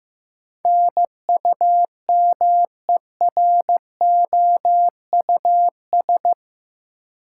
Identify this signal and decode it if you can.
Morse code, heard as a keyed tone, about 15 wpm, 700 Hz. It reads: NUMEROUS
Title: Sentences from Top 1400 Words - New Word Drill Alt - 15wpm